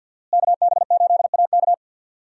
Morse code, heard as a keyed tone, about 51 wpm, 700 Hz. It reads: KB8AK
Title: USA Call Signs - 5 Characters - Repeated 3x - 50wpm